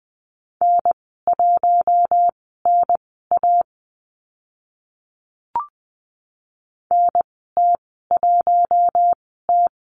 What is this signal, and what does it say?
Morse code, heard as a keyed tone, about 20 wpm, spaced as for 15 wpm, 700 Hz. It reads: N1NA NT1T
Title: 042 Lesson 8g - Number 1 - Call Signs 20wpm